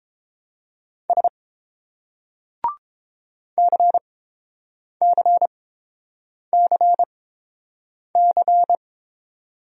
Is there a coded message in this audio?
Yes: Morse code from 1.1 to 8.8 s, heard as a keyed tone, about 30 wpm, spaced as for 7 wpm, 700 Hz. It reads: S CCCC